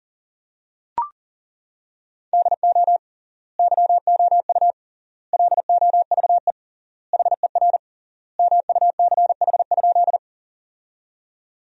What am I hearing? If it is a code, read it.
Morse code, heard as a keyed tone, about 40 wpm, 700 Hz. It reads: DO YOU LOVE HER MUCH?
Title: 3x Word-Spacing - Sentences from Top 400 Words - 40wpm